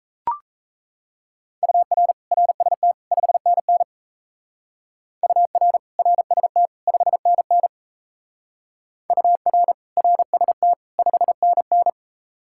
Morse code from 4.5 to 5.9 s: UR

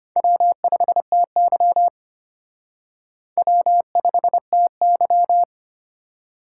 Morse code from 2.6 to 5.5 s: W5TY

